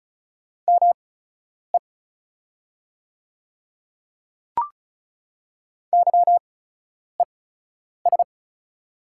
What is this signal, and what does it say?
Morse code, heard as a keyed tone, about 36 wpm, spaced as for 10 wpm, 690 Hz. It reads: ME YES
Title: 8x Character Spacing - Top 500 Words - 35wpm